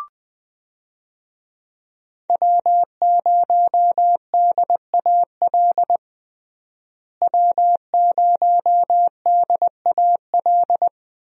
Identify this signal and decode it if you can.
Morse code, heard as a keyed tone, about 20 wpm, 700 Hz. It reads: W0DAL W0DAL
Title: USA Call Signs - 5 Characters - Repeated 3x - 20wpm